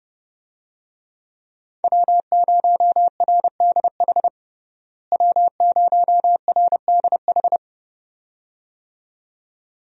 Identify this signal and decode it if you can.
Morse code, heard as a keyed tone, about 30 wpm, 700 Hz. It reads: W0RDH W0RDH
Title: USA Call Signs - 5 Characters - Repeated 2x - 30wpm